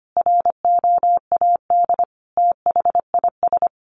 Morse code, heard as a keyed tone, about 25 wpm, 700 Hz. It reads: ROAD THIS
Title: Sentences from Top 1100 Words 25wpm